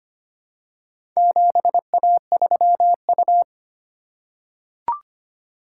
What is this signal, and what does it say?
Morse code, heard as a keyed tone, about 25 wpm, 700 Hz. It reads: 7A3U
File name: practice / 25wpm / Four Letter-Number Combinations - 25wpm